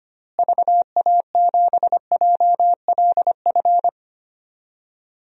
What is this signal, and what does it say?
Morse code, heard as a keyed tone, about 25 wpm, 700 Hz. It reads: VA7JLF
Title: DXCC-100 Call Signs - 6 Characters - Repeated 2x - 25wpm